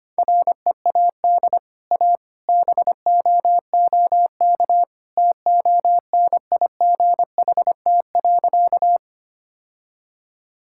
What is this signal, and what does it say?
Morse code, heard as a keyed tone, about 25 wpm, 700 Hz. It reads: READ A BOOK TONIGHT.
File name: Easy Sentences with Hints - 25wpm